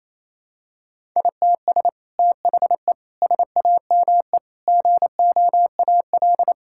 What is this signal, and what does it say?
Morse code, heard as a keyed tone, about 28 wpm, 700 Hz. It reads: ITS THE SAME GOAL
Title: Sentences from Top 1000 Words - Review Part 5 - 28wpm